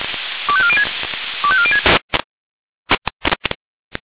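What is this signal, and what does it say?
Five-tone selective call (ZVEI), heard as quick runs of five tones, tones of about 70 ms. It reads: 25406 25406
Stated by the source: ZVEI five-tone selective call received on SDR.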